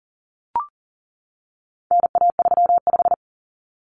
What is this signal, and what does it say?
Morse code, heard as a keyed tone, about 40 wpm, 700 Hz. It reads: NA35
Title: Common Cut Numbers - 4-Digit 40wpm